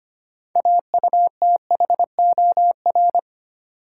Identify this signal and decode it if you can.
Morse code, heard as a keyed tone, about 25 wpm, 700 Hz. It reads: AUTHOR
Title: Sentences from Top 1200 Words - New Word Drill - 25wpm